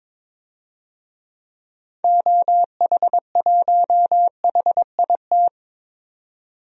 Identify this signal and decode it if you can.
Morse code, heard as a keyed tone, about 22 wpm, 700 Hz. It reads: OH1HIT
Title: DXCC-100 Call Signs - 6 Characters - 22wpm